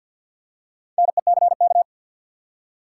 Morse code, heard as a keyed tone, about 50 wpm, 700 Hz. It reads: NECK
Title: Four Letter Words - 50wpm